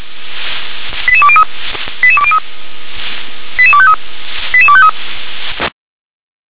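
Five-tone selective call (ZVEI), heard as quick runs of five tones, tones of about 70 ms. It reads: 88203 88203 88253 88253
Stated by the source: ZVEI five-tone call received by SDR.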